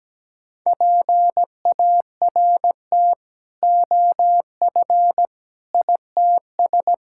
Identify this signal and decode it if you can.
Morse code, heard as a keyed tone, about 17 wpm, 700 Hz. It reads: PART OF ITS
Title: Sentences from Top 1600 Words 17wpm